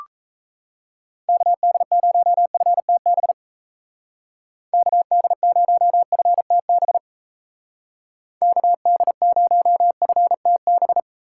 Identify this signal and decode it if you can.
Morse code, heard as a keyed tone, about 33 wpm, 700 Hz. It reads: KD0FTB KD0FTB KD0FTB
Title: Speed Racing - USA 2x3 Call Signs 28wpm